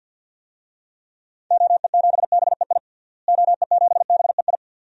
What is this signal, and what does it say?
Morse code, heard as a keyed tone, about 52 wpm, 700 Hz. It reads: OE7BEI OE7BEI
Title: DXCC-100 Call Signs - 6 Characters - Repeated 2x - 50wpm